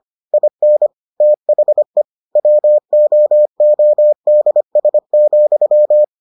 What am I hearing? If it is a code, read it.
Morse code, heard as a keyed tone, about 25 wpm, 600 Hz. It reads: IN THE WOODS,